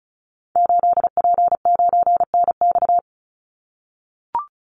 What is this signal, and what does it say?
Morse code, heard as a keyed tone, about 34 wpm, 700 Hz. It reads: 8P9NX
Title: Call Signs - 35wpm